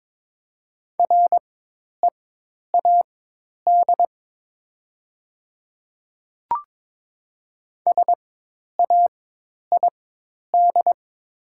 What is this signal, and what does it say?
Morse code, heard as a keyed tone, about 22 wpm, spaced as for 10 wpm, 700 Hz. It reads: READ SAID
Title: 4x Character Spacing - Top 200 Words - 22wpm